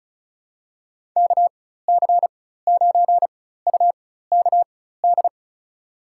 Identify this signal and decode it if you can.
Morse code, heard as a keyed tone, about 36 wpm, spaced as for 16 wpm, 700 Hz. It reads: KC9UKD
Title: USA Call Signs - 6 Characters - 4x Character Spacing - 35wpm